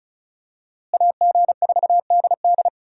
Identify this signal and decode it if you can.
Morse code, heard as a keyed tone, about 36 wpm, 700 Hz. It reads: AG4DD
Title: USA Call Signs - 5 Characters - Repeated 3x - 35wpm